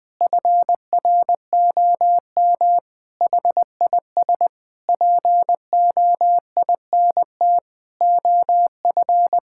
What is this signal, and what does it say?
Morse code, heard as a keyed tone, about 21 wpm, 700 Hz. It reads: FROM HIS POINT OF
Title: Sentences from Top 1000 Words - Review Part 6 - 20wpm